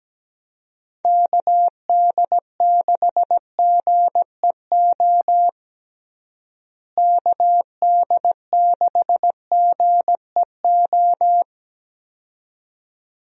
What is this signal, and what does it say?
Morse code, heard as a keyed tone, about 17 wpm, 700 Hz. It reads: KD6GEO KD6GEO
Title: USA Call Signs - 6 Characters - Repeated 2x - 17wpm